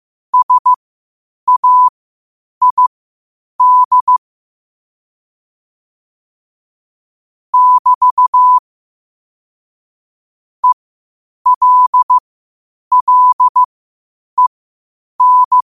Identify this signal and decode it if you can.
Morse code, heard as a keyed tone, about 15 wpm, spaced as for 8 wpm, 1000 Hz. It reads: SAID <BT> ELLEN
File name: qotdfiles / 8wpm